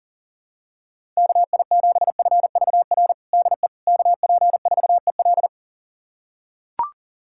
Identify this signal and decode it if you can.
Morse code, heard as a keyed tone, about 40 wpm, 700 Hz. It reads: KI7FVR DE KP4EL
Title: USA Callsign DE Callsign 40wpm